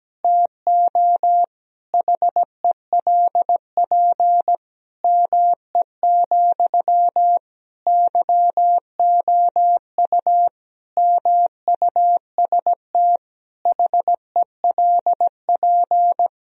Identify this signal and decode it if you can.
Morse code, heard as a keyed tone, about 17 wpm, 700 Hz. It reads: TO HELP ME, YOU MUST HELP
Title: Sentences from Top 1000 Words - Review Part 1 - 17wpm